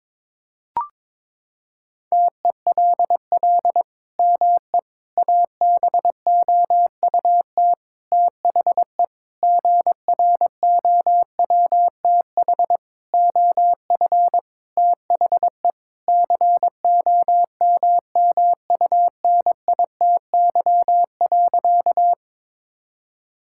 Morse code, heard as a keyed tone, about 21 wpm, 700 Hz. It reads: TELL ME ABOUT THE GROWTH OF THE COMMUNITY.
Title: Sentences from Top 1000 Words - Review Part 5 - 22wpm